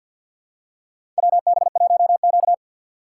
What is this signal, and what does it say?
Morse code, heard as a keyed tone, about 48 wpm, 700 Hz. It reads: WB1Q
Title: USA Call Signs - 4 Characters - 2x Character Spacing - 50wpm